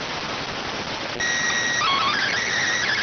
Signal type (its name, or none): SSTV (slow-scan television)